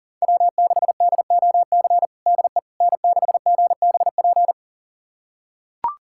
Morse code, heard as a keyed tone, about 37 wpm, 700 Hz. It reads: W6DOC DE N6GBP
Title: USA Callsign DE Callsign - 5 Characters - 40wpm